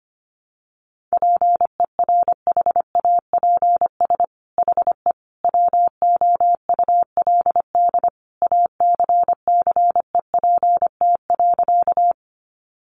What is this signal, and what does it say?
Morse code, heard as a keyed tone, about 25 wpm, 700 Hz. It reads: PERHAPS HE WOULD ACCEPT.
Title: Sentences from Top 1300 Words 25wpm